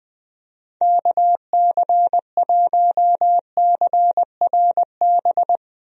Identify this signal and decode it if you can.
Morse code, heard as a keyed tone, about 21 wpm, 700 Hz. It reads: KC1CRB
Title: USA Call Signs - 6 Characters - 20wpm